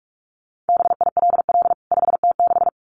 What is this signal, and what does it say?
Morse code, heard as a keyed tone, about 44 wpm, 700 Hz. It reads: BILL 5T6
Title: CWT Exchange 45wpm